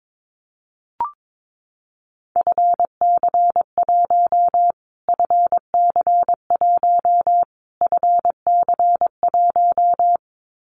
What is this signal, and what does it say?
Morse code, heard as a keyed tone, about 22 wpm, 700 Hz. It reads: FC1 FC1 FC1
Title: Callsign Prefixes for Top 25 DXCC - Repeated 3x - 22wpm